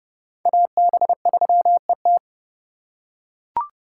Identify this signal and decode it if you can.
Morse code, heard as a keyed tone, about 30 wpm, 700 Hz. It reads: AB3ET
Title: USA Call Signs - 5 Characters - 4x Character Spacing - 30wpm